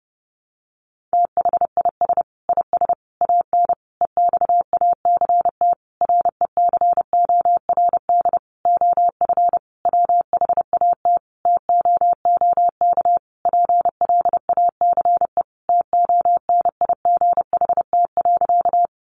Morse code, heard as a keyed tone, about 30 wpm, 700 Hz. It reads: THIS IS AN EXACT RECORD OF WHAT TOOK PLACE TONIGHT.
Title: Sentences from Top 1300 Words 30wpm